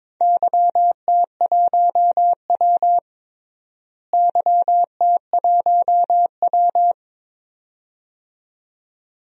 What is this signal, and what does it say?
Morse code, heard as a keyed tone, about 22 wpm, 700 Hz. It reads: YT1W YT1W
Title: DXCC-100 Call Signs - 4 Characters - Repeated 2x - 22wpm